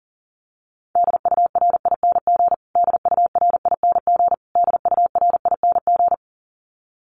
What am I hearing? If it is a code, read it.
Morse code, heard as a keyed tone, about 40 wpm, 700 Hz. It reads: DURING DURING DURING
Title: Top 500 Words Repeated 40wpm